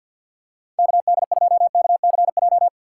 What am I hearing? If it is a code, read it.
Morse code, heard as a keyed tone, about 47 wpm, 700 Hz. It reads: KD2KCJ
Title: USA Call Signs - 6 Characters - 3x Character Spacing - 50wpm